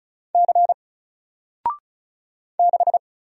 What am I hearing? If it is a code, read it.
Morse code, heard as a keyed tone, about 35 wpm, 700 Hz. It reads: C 6
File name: Single-Letter-Number-35wpm